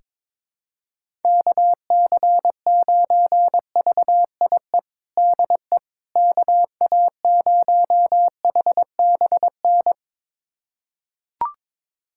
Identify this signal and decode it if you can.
Morse code, heard as a keyed tone, about 22 wpm, 700 Hz. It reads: KC9VIE DE KA0HBN
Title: USA Callsign DE Callsign - 6 Characters - 22wpm